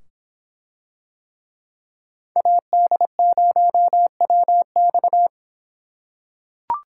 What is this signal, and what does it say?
Morse code, heard as a keyed tone, about 26 wpm, 700 Hz. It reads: AD0WX